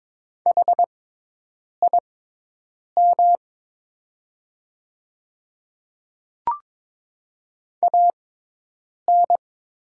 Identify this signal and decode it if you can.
Morse code, heard as a keyed tone, about 22 wpm, spaced as for 8 wpm, 700 Hz. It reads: HIM AN
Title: 6x Character Spacing - Top 200 Words - 22wpm